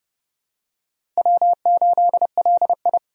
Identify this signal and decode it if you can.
Morse code, heard as a keyed tone, about 30 wpm, 700 Hz. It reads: W8LI
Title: USA Call Signs - 4 Characters - 30wpm